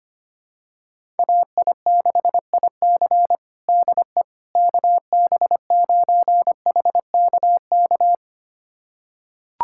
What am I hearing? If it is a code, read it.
Morse code, heard as a keyed tone, about 25 wpm, 700 Hz. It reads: AI6IC DE KB9HKK
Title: USA Callsign DE Callsign 25wpm